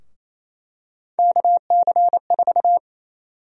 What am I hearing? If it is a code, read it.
Morse code, heard as a keyed tone, about 28 wpm, 700 Hz. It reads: KC4
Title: USA Call Sign Prefixes - Repeated 2x - 28wpm